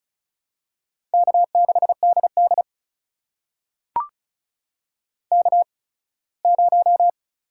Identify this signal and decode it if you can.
Morse code, heard as a keyed tone, about 34 wpm, spaced as for 21 wpm, 700 Hz. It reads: K6DD K0
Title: USA Call Signs - 4 Characters - 8x Character Spacing - 35wpm